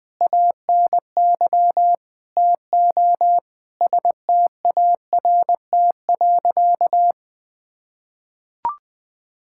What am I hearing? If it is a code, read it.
Morse code, heard as a keyed tone, about 20 wpm, 700 Hz. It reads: ANY TO START.
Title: Sentences with 8-Word Phrases - 20wpm